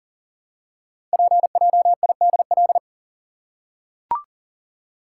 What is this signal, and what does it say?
Morse code, heard as a keyed tone, about 38 wpm, 690 Hz. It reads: PJIDL
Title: Code Groups - 5 Letters - 40wpm